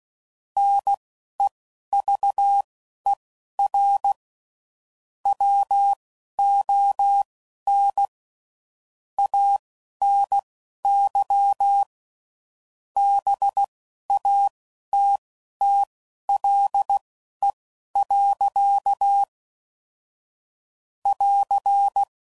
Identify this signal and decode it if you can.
Morse code, heard as a keyed tone, about 16 wpm, spaced as for 12 wpm, 800 Hz. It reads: NEVER WON ANY BATTLE. <AR>